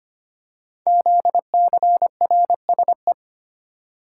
Morse code, heard as a keyed tone, about 25 wpm, 700 Hz. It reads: ZCRSE
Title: Code Groups - 5 Letters - 25wpm